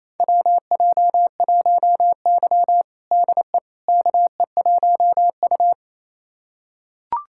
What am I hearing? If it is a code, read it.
Morse code, heard as a keyed tone, about 28 wpm, 700 Hz. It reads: WJ1Y DE KE1U
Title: USA Callsign DE Callsign - 4 Characters - 28wpm